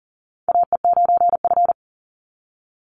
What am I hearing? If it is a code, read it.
Morse code, heard as a keyed tone, about 40 wpm, 700 Hz. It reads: AE9F
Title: NA SOTA Chaser Callsigns 40wpm